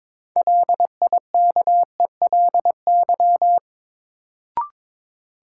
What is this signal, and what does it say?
Morse code, heard as a keyed tone, about 22 wpm, 700 Hz. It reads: LIKELY
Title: Sentences from Top 600 Words - New Word Drill - 22wpm